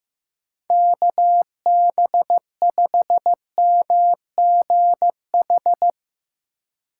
Morse code, heard as a keyed tone, about 15 wpm, 700 Hz. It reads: KB5MGH
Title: USA Call Signs - 6 Characters - 4x Character Spacing - 15wpm